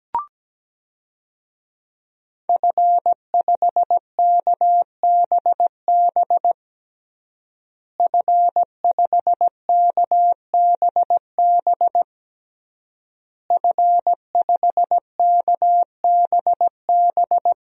Morse code, heard as a keyed tone, about 17 wpm, 700 Hz. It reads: F5KBB F5KBB F5KBB
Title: DXCC-100 Call Signs - 5 Characters - Repeated 3x - 17wpm